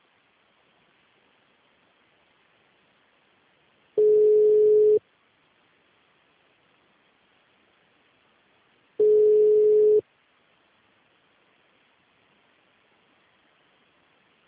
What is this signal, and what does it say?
Ringback tone (Europe, 425 Hz).